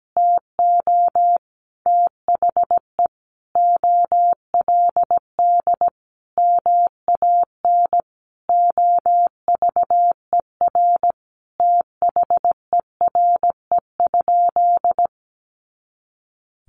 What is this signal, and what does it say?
Morse code, heard as a keyed tone, about 17 wpm, 700 Hz. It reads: TO THE OLD MAN OVER THERE?
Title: Sentences from Top 200 Words 17wpm